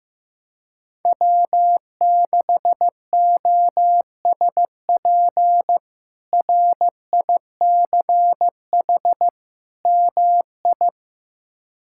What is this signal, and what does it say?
Morse code, heard as a keyed tone, about 15 wpm, 700 Hz. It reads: W6OSP RICH MI
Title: K1USN SST QSO Practice - 15wpm